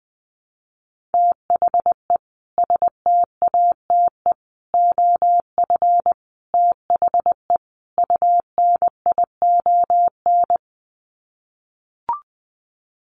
Morse code, heard as a keyed tone, about 19 wpm, 700 Hz. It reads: THE STATE OF THE UNION
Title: Sets of 5 Words Encore 20wpm